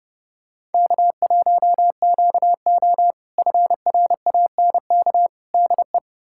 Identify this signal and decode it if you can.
Morse code, heard as a keyed tone, about 32 wpm, 700 Hz. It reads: K1QO FRANK DE
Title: K1USN SST QSO Practice - 30wpm